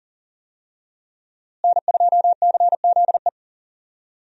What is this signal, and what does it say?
Morse code, heard as a keyed tone, about 40 wpm, 700 Hz. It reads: N2CZE
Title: USA Call Signs - 5 Characters - Repeated 2x - 40wpm